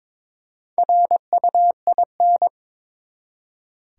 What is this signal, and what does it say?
Morse code, heard as a keyed tone, about 22 wpm, 700 Hz. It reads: RUIN
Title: Four Letter Words - 22wpm